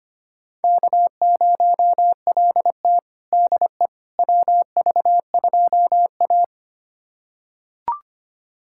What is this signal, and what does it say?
Morse code, heard as a keyed tone, about 25 wpm, 700 Hz. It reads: K0LT DE WV2A
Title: USA Callsign DE Callsign - 4 Characters - 25wpm